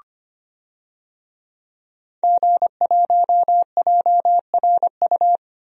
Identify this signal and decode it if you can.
Morse code, heard as a keyed tone, about 25 wpm, 700 Hz. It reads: G1JRU
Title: DXCC-100 Call Signs - 5 Characters - 25wpm